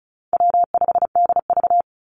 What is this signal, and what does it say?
Morse code, heard as a keyed tone, about 34 wpm, 700 Hz. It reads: W5DV
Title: Call Signs Repeated 2x - 35wpm